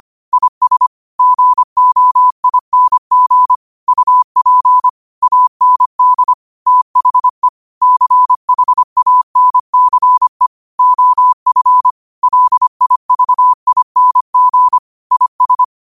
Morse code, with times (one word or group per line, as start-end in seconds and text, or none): 0.3-0.9 IS
1.2-3.5 GOING
3.9-4.9 UP
5.2-6.3 AND
6.7-7.5 THE
7.8-10.5 CHANCE
10.8-11.9 OF
12.2-14.8 LIVING
15.1-15.6 IS